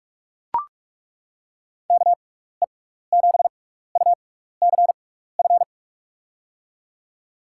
Morse code, heard as a keyed tone, about 42 wpm, spaced as for 15 wpm, 700 Hz. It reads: KE7UCF
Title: USA Call Signs - 6 Characters - 6x Character Spacing - 45wpm